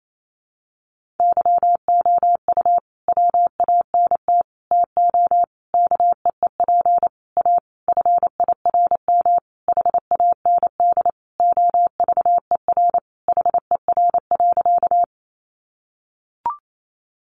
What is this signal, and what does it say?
Morse code, heard as a keyed tone, about 28 wpm, 700 Hz. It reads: YOU WANT TO KEEP A FIRM HAND OVER HER.
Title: Sentences from Top 1400 Words 28wpm